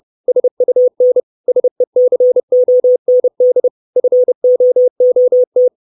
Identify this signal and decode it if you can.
Morse code, heard as a keyed tone, about 30 wpm, 500 Hz. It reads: SUN SECOND FOOT
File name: 632 Words 30WPM